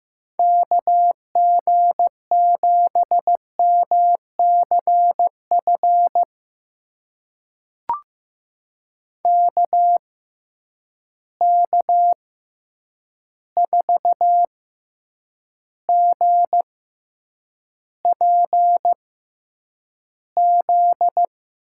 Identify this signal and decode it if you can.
Morse code, heard as a keyed tone, about 15 wpm, spaced as for 8 wpm, 700 Hz. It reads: KG7MCF KK4GPZ
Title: USA Call Signs - 6 Characters - 6x Character Spacing - 15wpm